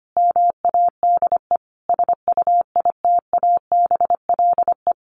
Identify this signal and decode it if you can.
Morse code, heard as a keyed tone, about 25 wpm, 700 Hz. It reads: MADE SUITABLE
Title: Sentences from Top 1400 Words 25wpm